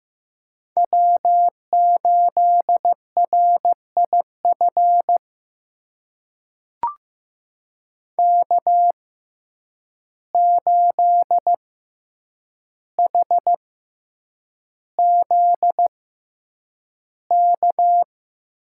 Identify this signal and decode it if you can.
Morse code, heard as a keyed tone, about 15 wpm, spaced as for 8 wpm, 700 Hz. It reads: W8RIF K8HZK
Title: USA Call Signs - 5 Characters - 6x Character Spacing - 15wpm